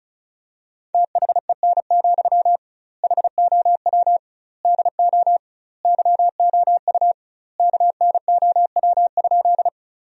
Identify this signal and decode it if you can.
Morse code, heard as a keyed tone, about 35 wpm, 700 Hz. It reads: THEN, HOW DO YOU KNOW?
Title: 2x Word-Spacing - Sentences from Top 100 Words - 35wpm